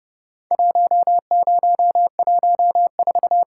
Morse code, heard as a keyed tone, about 30 wpm, 700 Hz. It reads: 1014
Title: Worked All Europe DX Contest - QTC Exchange - 30wpm